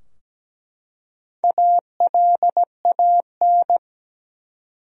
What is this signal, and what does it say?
Morse code, heard as a keyed tone, about 17 wpm, 700 Hz. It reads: ALAN
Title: USA Common Names - Max 4 Letters - 17wpm